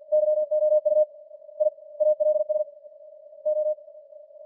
Morse code, heard as a keyed tone, about 46 wpm, spaced as for 27 wpm, 610 Hz. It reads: YOU I ALSO